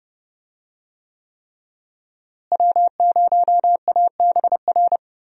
Morse code, heard as a keyed tone, about 31 wpm, 700 Hz. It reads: W0ABR
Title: USA Call Signs - 5 Characters - Repeated 3x - 30wpm